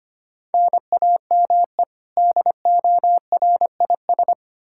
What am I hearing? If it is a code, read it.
Morse code, heard as a keyed tone, about 25 wpm, 700 Hz. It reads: NAME DORIS
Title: Practice QSO - Exchange 1 - Beginning - 25wpm